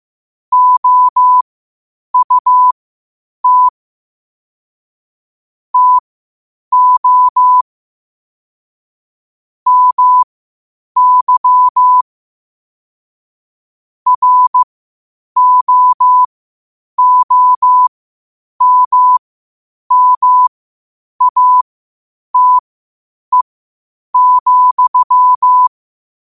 Morse code, heard as a keyed tone, about 15 wpm, spaced as for 8 wpm, 1000 Hz. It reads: OUT TO MY ROOMMATE,
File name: qotdfiles / 8wpm